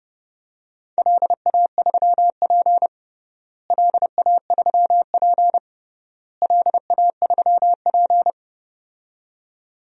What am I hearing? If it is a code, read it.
Morse code, heard as a keyed tone, about 30 wpm, 700 Hz. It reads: LA3P LA3P LA3P
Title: DXCC-100 Call Signs - 4 Characters - Repeated 3x - 30wpm